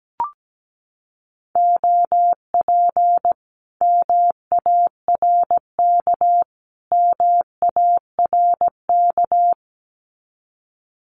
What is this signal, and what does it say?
Morse code, heard as a keyed tone, about 17 wpm, 700 Hz. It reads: OP MARK MARK
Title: Common QSO Phrases Encore 17wpm